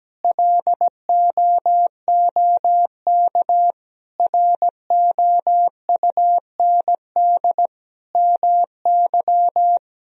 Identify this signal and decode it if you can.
Morse code, heard as a keyed tone, about 17 wpm, 700 Hz. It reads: LOOK ROUND MY
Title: Sentences from Top 200 Words and Abbreviations 17wpm